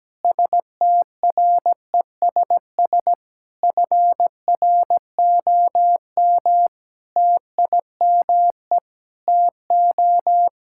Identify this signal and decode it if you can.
Morse code, heard as a keyed tone, about 17 wpm, 700 Hz. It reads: STRESS FROM TIME TO
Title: Sentences from Top 1000 Words - Review Part 6 - 17wpm